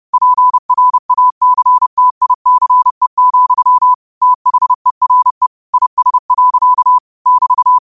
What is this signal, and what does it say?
Morse code, heard as a keyed tone, about 30 wpm, 1000 Hz. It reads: PRACTICE, THERE IS. <BT>